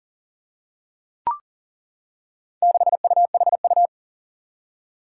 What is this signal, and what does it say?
Morse code, heard as a keyed tone, about 40 wpm, 700 Hz. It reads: 6UHU
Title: Four Challenging Letter-Number Combinations 40wpm